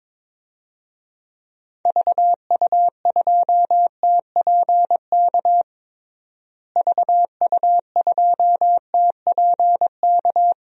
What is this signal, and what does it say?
Morse code, heard as a keyed tone, about 22 wpm, 700 Hz. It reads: VU2TPK VU2TPK
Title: DXCC-100 Call Signs - 6 Characters - Repeated 3x - 22wpm